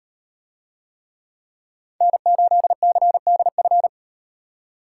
Morse code, heard as a keyed tone, about 38 wpm, 700 Hz. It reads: N8CDF